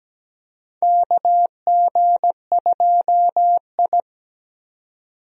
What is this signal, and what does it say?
Morse code, heard as a keyed tone, about 17 wpm, 700 Hz. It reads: KG2I
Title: USA Call Signs - 4 Characters - 17wpm